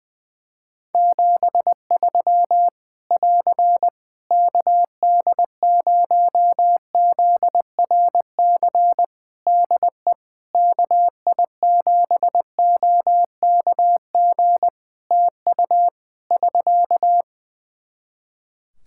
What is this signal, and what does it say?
Morse code, heard as a keyed tone, about 20 wpm, 700 Hz. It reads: 73 <AR> KD0ZRC DE KI7OKG TU <SK>
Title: K1USN SST QSO Practice 20wpm